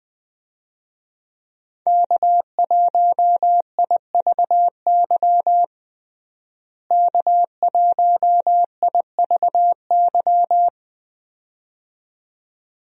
Morse code, heard as a keyed tone, about 20 wpm, 700 Hz. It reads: K1IVY K1IVY
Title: USA Call Signs - 5 Characters - Repeated 2x - 20wpm